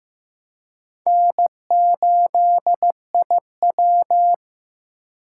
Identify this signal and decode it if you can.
Morse code, heard as a keyed tone, about 15 wpm, 700 Hz. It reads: N8IW
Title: USA Call Signs - 4 Characters - 4x Character Spacing - 15wpm